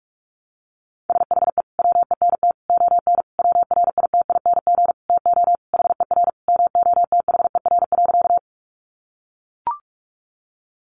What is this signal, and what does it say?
Morse code, heard as a keyed tone, about 44 wpm, 700 Hz. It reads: SHE WENT ON WRITING TO HER MOTHER.
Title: Sentences from Top 800 Words 45wpm